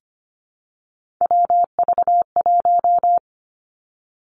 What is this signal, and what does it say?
Morse code, heard as a keyed tone, about 25 wpm, 700 Hz. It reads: WV1
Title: Three Challenging Letter-Number Combinations 25wpm